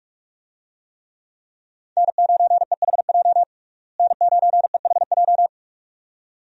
Morse code, heard as a keyed tone, about 42 wpm, 700 Hz. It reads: N9EHJ N9EHJ
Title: USA Call Signs - 5 Characters - Repeated 2x - 45wpm